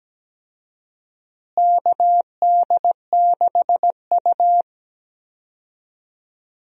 Morse code, heard as a keyed tone, about 17 wpm, 700 Hz. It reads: KD6U
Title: USA Call Signs - 4 Characters - Repeated 3x - 17wpm